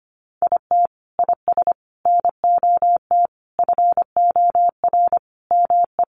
Morse code, heard as a keyed tone, about 25 wpm, 700 Hz. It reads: IT IS NOT FOR ME
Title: Sentences from Top 1500 Words 25wpm